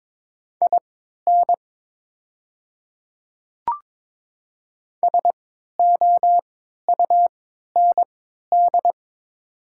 Morse code, heard as a keyed tone, about 22 wpm, spaced as for 13 wpm, 700 Hz. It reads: IN SOUND
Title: 3x Character Spacing - Top 200 Words - 22wpm